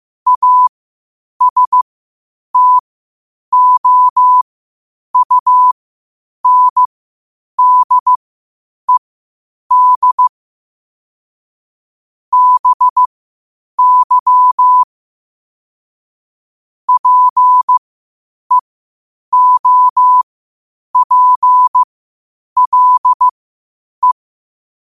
Morse code, heard as a keyed tone, about 15 wpm, spaced as for 8 wpm, 1000 Hz. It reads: ASTOUNDED BY PEOPLE
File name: qotdfiles / 8wpm